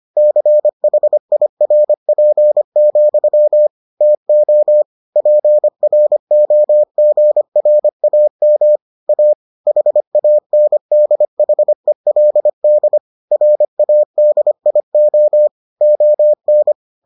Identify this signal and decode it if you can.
Morse code, heard as a keyed tone, about 25 wpm, 600 Hz. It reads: CHIRP, TO PROGRAM A HANDHELD RADIO ON